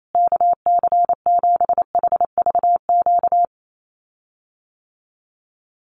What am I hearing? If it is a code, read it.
Morse code, heard as a keyed tone, about 28 wpm, 700 Hz. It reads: KC7HVQ
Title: USA Call Signs Repeated 3x 28wpm